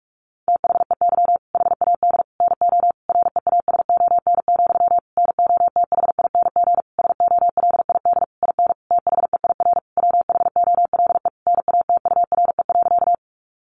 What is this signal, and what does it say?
Morse code, heard as a keyed tone, about 45 wpm, 700 Hz. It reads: THEY HAD NO REASON, NOTHING SOLID IN THEIR WHOLE NATURE.
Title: Sentences from Top 1500 Words 45wpm